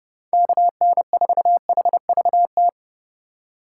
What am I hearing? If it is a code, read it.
Morse code, heard as a keyed tone, about 37 wpm, 700 Hz. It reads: KN4HVT
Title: Speed Racing - USA 2x3 Call Signs 30wpm